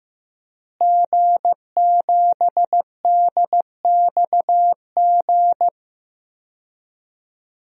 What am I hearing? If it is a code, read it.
Morse code, heard as a keyed tone, about 15 wpm, 700 Hz. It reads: G7DXG
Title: DXCC-100 Call Signs - 5 Characters - Repeated 3x - 15wpm